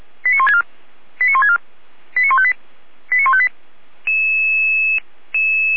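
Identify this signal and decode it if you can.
Five-tone selective call (ZVEI), heard as quick runs of five tones, tones of about 80 ms. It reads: 89164 89164 89158 89158
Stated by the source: ZVEI five-tone selective call 89160 received on SDR, from the Feuerwehr Orlamünde (fire brigade) audio server.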